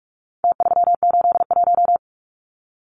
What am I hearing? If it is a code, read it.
Morse code, heard as a keyed tone, about 44 wpm, 700 Hz. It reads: T381
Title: Numbers - Contest Style - 4-Digit 45wpm